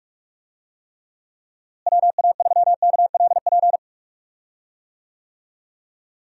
Morse code, heard as a keyed tone, about 41 wpm, 700 Hz. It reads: WA3KLP
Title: USA Call Signs - 6 Characters - Repeated 2x - 45wpm